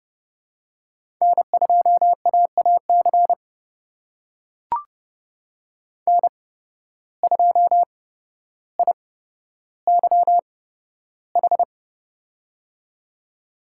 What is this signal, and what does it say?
Morse code, heard as a keyed tone, about 30 wpm, spaced as for 13 wpm, 700 Hz. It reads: N2AAC N2IYH
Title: USA Call Signs - 5 Characters - 8x Character Spacing - 30wpm